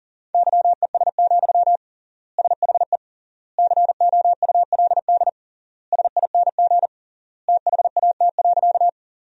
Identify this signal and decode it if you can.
Morse code, heard as a keyed tone, about 40 wpm, 700 Hz. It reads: YES, SHE COULD SING THAT.
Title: 3x Word-Spacing - Sentences from Top 500 Words - 40wpm